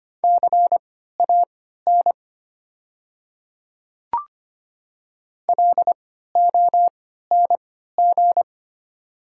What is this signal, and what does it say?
Morse code, heard as a keyed tone, about 25 wpm, spaced as for 14 wpm, 690 Hz. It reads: CAN LONG
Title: 3x Character Spacing - Top 200 Words - 25wpm